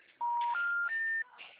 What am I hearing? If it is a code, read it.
Special information tone.